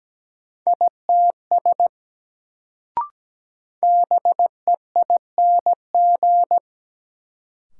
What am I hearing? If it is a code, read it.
Morse code, heard as a keyed tone, about 17 wpm, 700 Hz. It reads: ITS BEING
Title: Sentences from Top 600 Words - New Word Drill - 17wpm